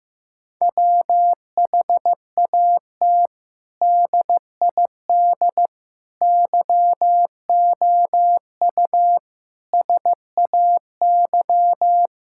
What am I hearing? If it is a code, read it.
Morse code, heard as a keyed tone, about 15 wpm, 700 Hz. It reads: WHAT DID YOU SAY
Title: Sentences from Top 1000 Words - Review Part 6 - 15wpm